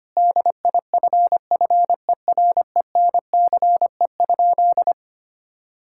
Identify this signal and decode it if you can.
Morse code, heard as a keyed tone, about 25 wpm, 700 Hz. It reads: DIFFERENCE?
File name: practice / 25wpm / Sentences with 2-Word Phrases - 25wpm